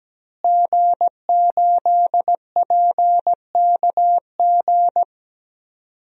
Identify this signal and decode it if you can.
Morse code, heard as a keyed tone, about 17 wpm, 700 Hz. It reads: G8PKG
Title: DXCC-100 Call Signs - 5 Characters - Repeated 3x - 17wpm